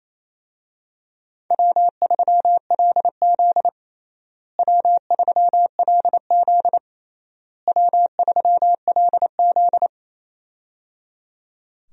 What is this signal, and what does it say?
Morse code, heard as a keyed tone, about 28 wpm, 700 Hz. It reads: W3LZ W3LZ W3LZ
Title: USA Call Signs - 4 Characters - Repeated 3x - 28wpm